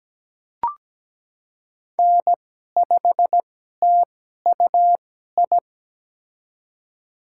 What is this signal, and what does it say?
Morse code, heard as a keyed tone, about 17 wpm, spaced as for 12 wpm, 700 Hz. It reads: N5TUI
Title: USA Call Signs - 5 Characters - 2x Character Spacing - 17wpm